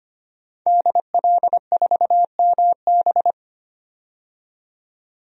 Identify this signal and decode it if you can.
Morse code, heard as a keyed tone, about 25 wpm, 700 Hz. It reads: DL4MB
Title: DXCC-100 Call Signs - 5 Characters - 25wpm